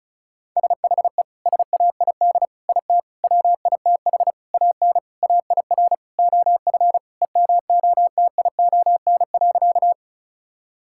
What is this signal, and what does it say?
Morse code, heard as a keyed tone, about 34 wpm, 700 Hz. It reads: SHE SAID IT WITH AN AIR OF EMOTION.
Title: Sentences from Top 1600 Words 35wpm